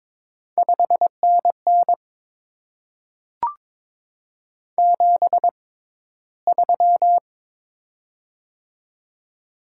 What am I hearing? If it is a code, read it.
Morse code, heard as a keyed tone, about 22 wpm, spaced as for 14 wpm, 700 Hz. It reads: 5NN 73
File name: Top 20 QSO Elements - 6x Character Spacing - 22wpm